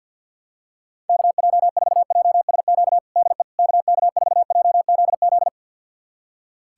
Morse code, heard as a keyed tone, about 46 wpm, 700 Hz. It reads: KJ4JSQ DE KK4JZZ
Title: USA Callsign DE Callsign 50wpm